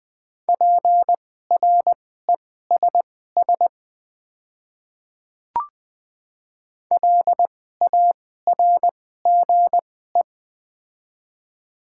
Morse code, heard as a keyed tone, about 20 wpm, spaced as for 15 wpm, 700 Hz. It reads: PRESS LARGE
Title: 2x Character Spacing - Top 300 Words - 20wpm